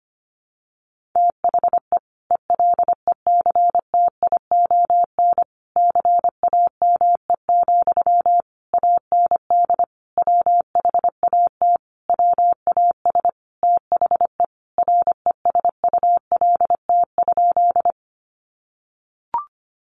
Morse code, heard as a keyed tone, about 25 wpm, 700 Hz. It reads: THE ELECTION CAME, AND WHAT WAS THE RESULT?
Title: Sentences from Top 1500 Words 25wpm